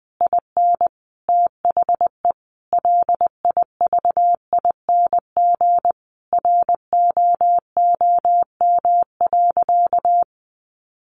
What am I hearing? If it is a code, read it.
Morse code, heard as a keyed tone, about 20 wpm, 700 Hz. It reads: IN THE LIVING ROOM.
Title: Sentences from Top 1200 Words 20wpm